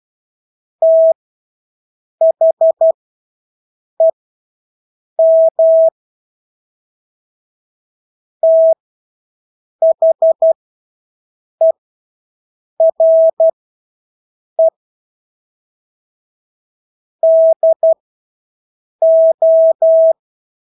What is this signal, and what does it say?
Morse code, heard as a keyed tone, about 12 wpm, spaced as for 6 wpm, 650 Hz. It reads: THEM THERE DO
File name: Engelske ord 12-6 wpm